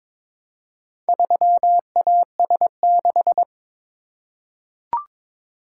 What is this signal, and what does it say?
Morse code, heard as a keyed tone, about 22 wpm, 700 Hz. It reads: 3AS6
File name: practice / 22wpm / Four Letter-Number Combinations - 22wpm